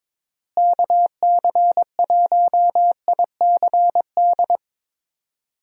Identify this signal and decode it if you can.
Morse code, heard as a keyed tone, about 22 wpm, 700 Hz. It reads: KC1ICD
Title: USA Call Signs - 6 Characters - Repeated 2x - 22wpm